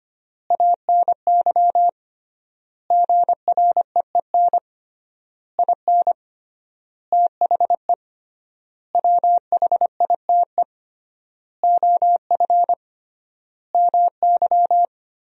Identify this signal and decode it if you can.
Morse code, heard as a keyed tone, about 25 wpm, 700 Hz. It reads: ANY GREEN IN THE WHITE OF MY
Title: 3x Word-Spacing - Sentences from Top 500 Words - 25wpm